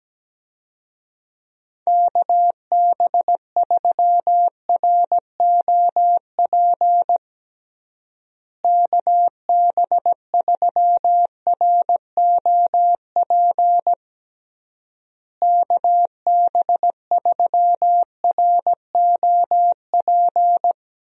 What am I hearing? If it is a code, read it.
Morse code, heard as a keyed tone, about 17 wpm, 700 Hz. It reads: KB3ROP KB3ROP KB3ROP
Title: USA Call Signs - 6 Characters - Repeated 3x - 17wpm